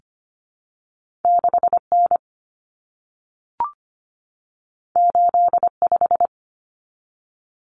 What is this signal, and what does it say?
Morse code, heard as a keyed tone, about 25 wpm, 700 Hz. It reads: 6 N 8 5
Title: Common Cut Numbers - 2-Digit 25wpm